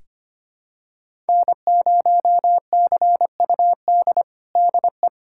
Morse code, heard as a keyed tone, about 25 wpm, 700 Hz. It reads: N0CUD DE
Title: USA Callsign DE Callsign - 5 Characters - 25wpm